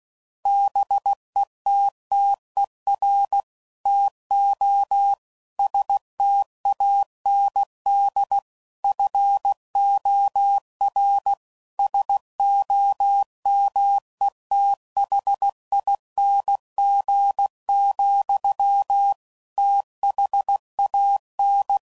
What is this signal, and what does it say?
Morse code, heard as a keyed tone, about 16 wpm, 800 Hz. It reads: BETTER TO STAND FOR SOMETHING, THAN